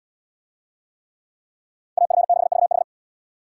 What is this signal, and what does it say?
Morse code, heard as a keyed tone, about 75 wpm, 700 Hz. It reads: AF6VH